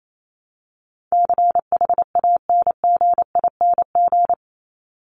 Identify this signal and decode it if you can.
Morse code, heard as a keyed tone, about 28 wpm, 700 Hz. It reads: CHANGING
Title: ING Suffix - Top 100 Words with Base Words 28wpm